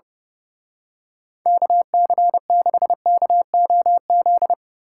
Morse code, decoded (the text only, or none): KC6KOZ